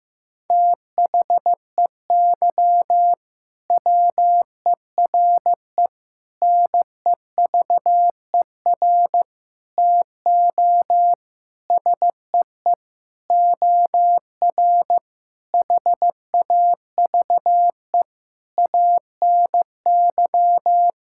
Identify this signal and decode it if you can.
Morse code, heard as a keyed tone, about 15 wpm, 700 Hz. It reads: THEY WERE NEVER TO SEE OR HAVE ANY
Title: Sentences from Top 1000 Words - Review Part 6 - 15wpm